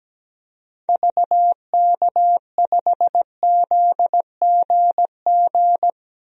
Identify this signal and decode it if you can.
Morse code, heard as a keyed tone, about 17 wpm, 700 Hz. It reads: VK5ZGG
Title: DXCC-100 Call Signs - 6 Characters - 17wpm